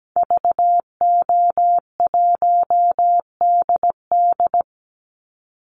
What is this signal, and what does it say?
Morse code, heard as a keyed tone, about 17 wpm, 700 Hz. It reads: VO1DD
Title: Call Signs Repeated 3x - 17wpm